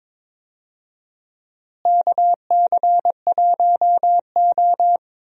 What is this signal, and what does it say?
Morse code, heard as a keyed tone, about 22 wpm, 700 Hz. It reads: KC1O
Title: USA Call Signs - 4 Characters - Repeated 2x - 22wpm